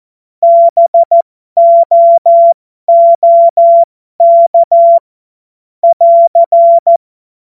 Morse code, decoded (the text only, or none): BOOK <AR>